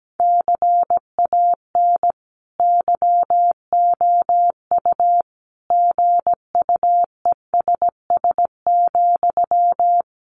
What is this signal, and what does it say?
Morse code, heard as a keyed tone, about 17 wpm, 700 Hz. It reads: CAN YOU GUESS,
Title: Sentences from Top 1500 Words 17wpm